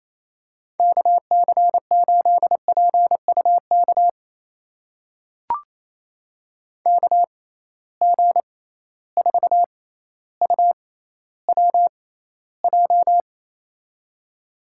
Morse code, heard as a keyed tone, about 28 wpm, spaced as for 14 wpm, 700 Hz. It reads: KC8PUK KG4UWJ
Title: USA Call Signs - 6 Characters - 6x Character Spacing - 28wpm